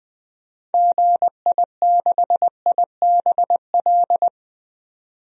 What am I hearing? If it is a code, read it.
Morse code, heard as a keyed tone, about 20 wpm, 700 Hz. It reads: GI6IBL
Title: DXCC-100 Call Signs - 6 Characters - 20wpm